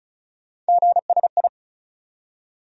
Morse code, heard as a keyed tone, about 36 wpm, 700 Hz. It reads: GSI
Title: USA Call Sign Postfixes - 3 Letters 35wpm